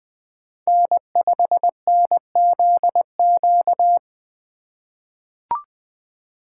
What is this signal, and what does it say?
Morse code, heard as a keyed tone, about 20 wpm, 700 Hz. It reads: N5NZQ
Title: USA Call Signs - 5 Characters - 4x Character Spacing - 20wpm